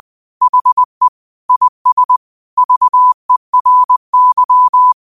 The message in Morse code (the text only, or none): HE IS VERY